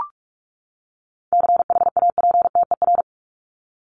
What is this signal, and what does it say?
Morse code, heard as a keyed tone, about 44 wpm, 700 Hz. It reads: CHAPTER
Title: Top 1751-2000 Words 45wpm